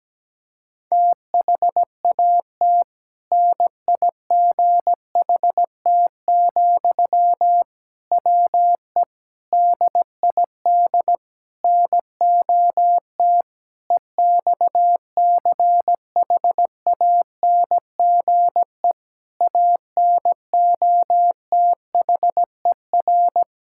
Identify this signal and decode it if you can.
Morse code, heard as a keyed tone, about 17 wpm, 700 Hz. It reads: THAT NIGHT, WE DID NOT EXCHANGE ANOTHER
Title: Sentences from Top 1000 Words - Review Part 6 - 17wpm